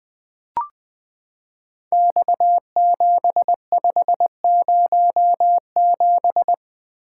Morse code, heard as a keyed tone, about 20 wpm, 700 Hz. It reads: X7507
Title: Code Groups - 5 Letter:Numbers - 20wpm